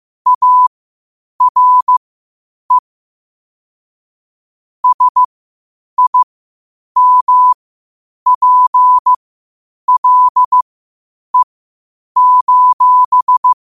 Morse code, heard as a keyed tone, about 15 wpm, spaced as for 8 wpm, 1000 Hz. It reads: ARE SIMPLE:
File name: qotdfiles / 8wpm